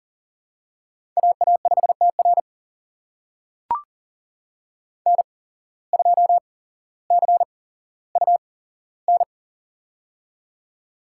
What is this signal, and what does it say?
Morse code, heard as a keyed tone, about 40 wpm, spaced as for 17 wpm, 700 Hz. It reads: AA5TR N2CUN
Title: USA Call Signs - 5 Characters - 8x Character Spacing - 40wpm